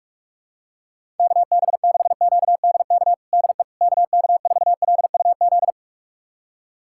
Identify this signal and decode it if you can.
Morse code, heard as a keyed tone, about 46 wpm, 700 Hz. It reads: KB6QDK DE KK4LUZ
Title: USA Callsign DE Callsign 45wpm